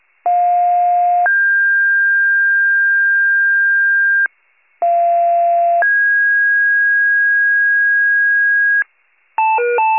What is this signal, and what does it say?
Two-tone page, heard as a one-second tone, then a three-second tone, about 710 then 1600 Hz, about 690 then 1650 Hz. It is heard from 0.3 to 8.8 s.